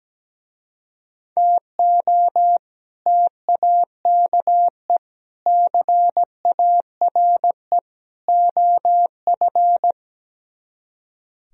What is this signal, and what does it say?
Morse code, heard as a keyed tone, about 17 wpm, 700 Hz. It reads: TO TAKE CARE OF
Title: Speed-Racing - Sets of 4 Words 17wpm